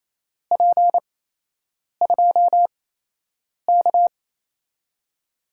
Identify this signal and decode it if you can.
Morse code, heard as a keyed tone, about 28 wpm, spaced as for 8 wpm, 700 Hz. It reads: P2K
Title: USA Call Signs - 4 Characters - 8x Character Spacing - 28wpm